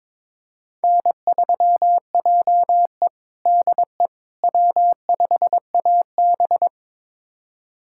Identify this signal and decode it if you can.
Morse code, heard as a keyed tone, about 22 wpm, 700 Hz. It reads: N3JE DE W5AB
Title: USA Callsign DE Callsign - 4 Characters - 22wpm